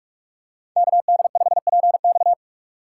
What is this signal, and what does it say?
Morse code, heard as a keyed tone, about 40 wpm, 700 Hz. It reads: KD5PX